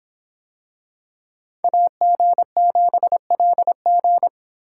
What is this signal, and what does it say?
Morse code, heard as a keyed tone, about 24 wpm, 700 Hz. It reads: AG7LG